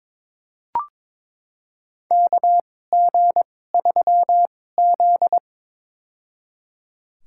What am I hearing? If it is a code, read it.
Morse code, heard as a keyed tone, about 22 wpm, spaced as for 16 wpm, 700 Hz. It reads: KG3Z
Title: USA Call Signs - 2x Character Spacing - 22wpm